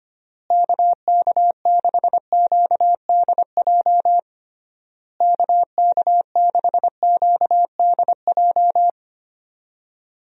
Morse code, heard as a keyed tone, about 25 wpm, 700 Hz. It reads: KK6QDJ KK6QDJ
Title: USA Call Signs - 6 Characters - Repeated 3x - 25wpm